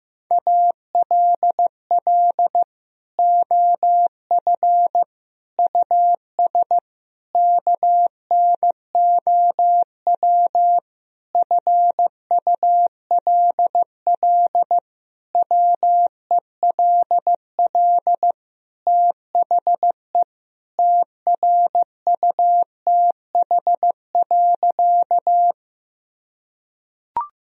Morse code, heard as a keyed tone, about 16 wpm, 700 Hz. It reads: ALL OF US KNOW FULL WELL THE TRUTH.
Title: Sentences from Top 1000 Words - Review Part 3 - 15wpm